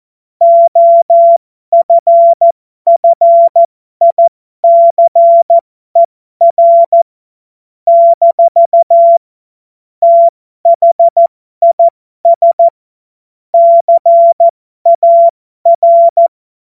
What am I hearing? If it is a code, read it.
Morse code, heard as a keyed tone, about 14 wpm, spaced as for 12 wpm, 680 Hz. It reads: OFFICER - THIS CAR